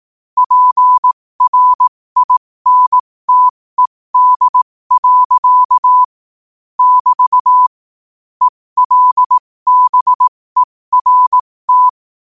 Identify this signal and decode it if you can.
Morse code, heard as a keyed tone, about 18 wpm, spaced as for 15 wpm, 1000 Hz. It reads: PRINTED. <BT> ELBERT
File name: qotdfiles / 15wpm